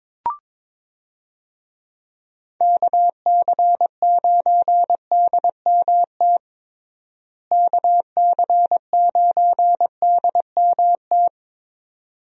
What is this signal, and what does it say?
Morse code, heard as a keyed tone, about 22 wpm, 700 Hz. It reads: KC9DMT KC9DMT
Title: USA Call Signs - 6 Characters - Repeated 2x - 22wpm